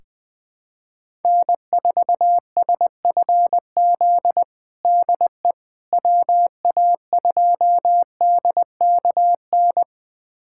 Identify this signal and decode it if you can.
Morse code, heard as a keyed tone, about 21 wpm, 700 Hz. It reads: N4SFZ DE WA2DKN
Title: USA Callsign DE Callsign 20wpm